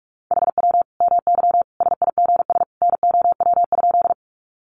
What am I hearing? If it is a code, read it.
Morse code, heard as a keyed tone, about 44 wpm, 700 Hz. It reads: HW MY SIGS NOW?
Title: Common QSO Phrases 45wpm